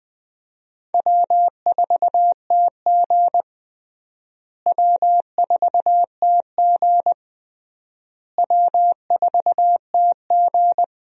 Morse code, heard as a keyed tone, about 20 wpm, 700 Hz. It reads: W4TG W4TG W4TG
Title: USA Call Signs - 4 Characters - Repeated 3x - 20wpm